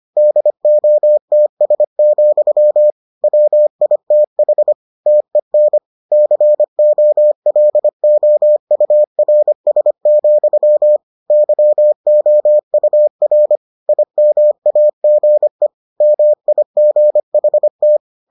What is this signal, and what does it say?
Morse code, heard as a keyed tone, about 25 wpm, 600 Hz. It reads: DOTS, WITH TEN COLOURS, YOUR IMAGE MIGHT